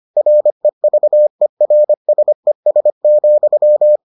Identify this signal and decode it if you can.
Morse code, heard as a keyed tone, about 25 wpm, 600 Hz. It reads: REVERSES,